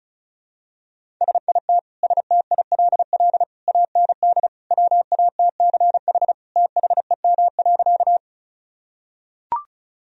Morse code, heard as a keyed tone, about 34 wpm, 700 Hz. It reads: SIT STILL AND WATCH THEM.
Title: Sentences from Top 1000 Words - Review Part 1 - 35wpm